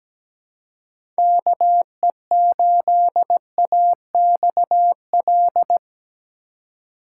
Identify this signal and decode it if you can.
Morse code, heard as a keyed tone, about 17 wpm, 700 Hz. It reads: KE8AXL
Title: USA Call Signs - 6 Characters - 2x Character Spacing - 17wpm